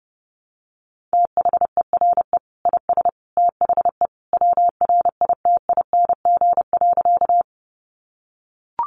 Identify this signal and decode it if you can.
Morse code, heard as a keyed tone, about 30 wpm, 700 Hz. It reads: THERE IS THE WRITING.
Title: Sentences from Top 800 Words 30wpm